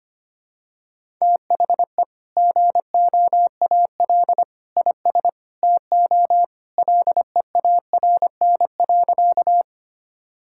Morse code, heard as a keyed tone, about 25 wpm, 700 Hz. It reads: THE GOAL IS TO LEARN.
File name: practice / 25wpm / Sentences with 2-Word Phrases - 25wpm